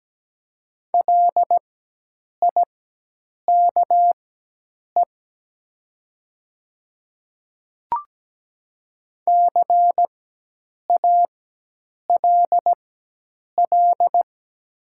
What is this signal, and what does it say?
Morse code, heard as a keyed tone, about 17 wpm, spaced as for 8 wpm, 700 Hz. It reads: LIKE CALL
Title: 4x Character Spacing - Top 200 Words - 17wpm